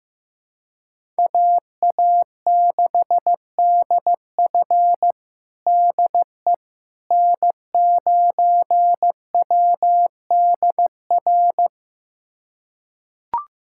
Morse code, heard as a keyed tone, about 15 wpm, 700 Hz. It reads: AA6DF DE N9WDR
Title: USA Callsign DE Callsign - 5 Characters - 15wpm